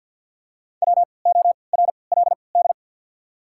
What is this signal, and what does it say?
Morse code, heard as a keyed tone, about 53 wpm, spaced as for 29 wpm, 700 Hz. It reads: WORLD
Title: 3x Character Spacing - Top 500 Words - 50wpm